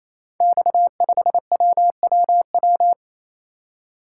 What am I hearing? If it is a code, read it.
Morse code, heard as a keyed tone, about 27 wpm, 700 Hz. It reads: X5WWW
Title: Code Groups - 5 Letter:Numbers - 28wpm